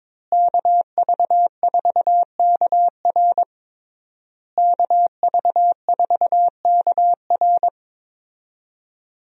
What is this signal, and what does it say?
Morse code, heard as a keyed tone, about 22 wpm, 700 Hz. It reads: KV4KR KV4KR
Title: USA Call Signs - 5 Characters - Repeated 3x - 22wpm